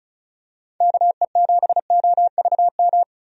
Morse code, heard as a keyed tone, about 34 wpm, 700 Hz. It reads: KE7OVM
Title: USA Call Signs - 6 Characters - 35wpm